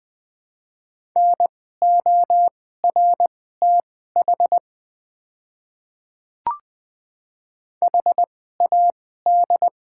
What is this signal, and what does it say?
Morse code, heard as a keyed tone, about 20 wpm, spaced as for 14 wpm, 700 Hz. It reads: NORTH HAD
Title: 2x Character Spacing - Top 500 Words - 20wpm